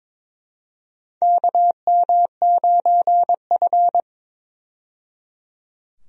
Morse code, heard as a keyed tone, about 22 wpm, 700 Hz. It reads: KM9F
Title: USA Call Signs - 4 Characters - 22wpm